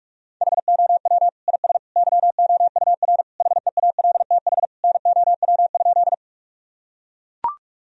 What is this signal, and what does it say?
Morse code, heard as a keyed tone, about 41 wpm, 700 Hz. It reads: HOW IS YOUR HEALTH NOW?
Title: Sentences from Top 1000 Words - Review Part 3 - 45wpm